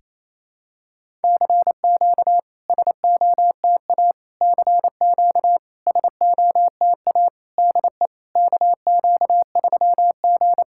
Morse code, heard as a keyed tone, about 28 wpm, 690 Hz. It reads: CQ SOTA CQ SOTA DE KQ3G
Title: SOTA QSO Practice 28wpm